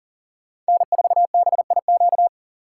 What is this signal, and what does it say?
Morse code, heard as a keyed tone, about 40 wpm, 700 Hz. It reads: N4BIQ
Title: USA Call Signs - 5 Characters - 6x Character Spacing - 40wpm